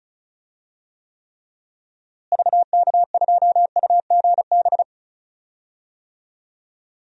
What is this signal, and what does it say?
Morse code, heard as a keyed tone, about 36 wpm, 700 Hz. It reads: VK2UGB
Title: DXCC-100 Call Signs - 6 Characters - Repeated 3x - 35wpm